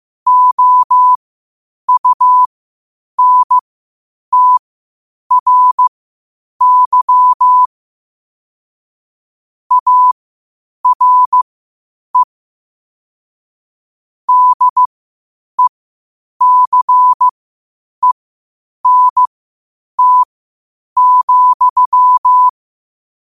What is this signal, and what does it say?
Morse code, heard as a keyed tone, about 15 wpm, spaced as for 8 wpm, 1000 Hz. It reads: OUNTRY ARE DECENT,
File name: qotdfiles / 8wpm